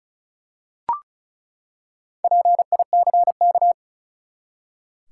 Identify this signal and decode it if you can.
Morse code, heard as a keyed tone, about 34 wpm, 700 Hz. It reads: PICK
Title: Four Letter Words - 35wpm